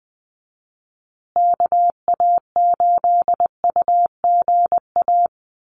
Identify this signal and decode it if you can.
Morse code, heard as a keyed tone, about 20 wpm, 700 Hz. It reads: KA8UGA
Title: USA Call Signs Repeated 2x 20wpm